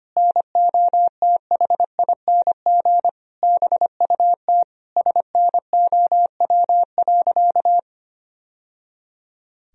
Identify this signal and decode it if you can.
Morse code, heard as a keyed tone, about 25 wpm, 700 Hz. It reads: NOTHING BUT SNOW.
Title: Sentences from Top 1000 Words - Review Part 3 - 25wpm